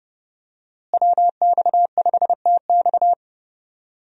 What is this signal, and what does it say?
Morse code, heard as a keyed tone, about 30 wpm, 700 Hz. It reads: WX5TX
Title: USA Call Signs - 5 Characters - Repeated 3x - 30wpm